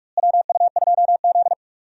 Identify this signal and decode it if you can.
Morse code, heard as a keyed tone, about 45 wpm, 700 Hz. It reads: WU2Z
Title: USA Call Signs - 4 Characters - Repeated 3x - 45wpm